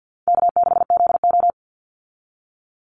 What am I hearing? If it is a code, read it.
Morse code, heard as a keyed tone, about 49 wpm, 700 Hz. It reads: K6ZO
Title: Call Signs - 50wpm